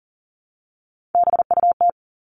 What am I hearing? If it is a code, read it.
Morse code, heard as a keyed tone, about 40 wpm, 700 Hz. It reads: BUT
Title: Top 300 Words 40wpm